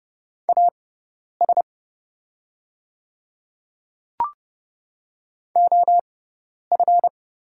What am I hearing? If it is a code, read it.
Morse code, heard as a keyed tone, about 29 wpm, spaced as for 10 wpm, 690 Hz. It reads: AS OF
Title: 6x Character Spacing - Top 100 Words - 30wpm